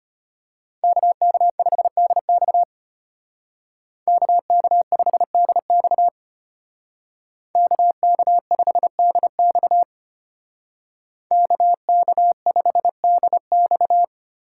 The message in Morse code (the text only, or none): KK5DX KK5DX KK5DX KK5DX